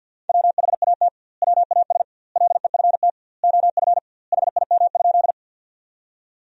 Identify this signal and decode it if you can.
Morse code, heard as a keyed tone, about 50 wpm, 700 Hz. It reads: WHAT WAS LEFT OF HIM?
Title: 2x Word-Spacing - Sentences from Top 300 Words - 50wpm